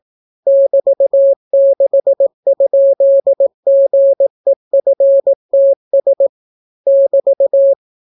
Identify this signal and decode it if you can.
Morse code, heard as a keyed tone, about 18 wpm, 550 Hz. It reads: <BT>6?GEFTS <BT>